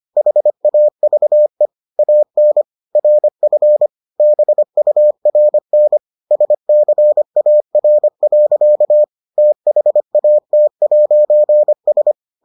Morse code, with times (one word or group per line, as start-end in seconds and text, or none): 0.2-1.7 HAVE
2.0-2.6 AN
3.0-3.9 RF
4.2-6.0 BURN
6.3-9.0 SCAR.
9.4-12.1 THAT'S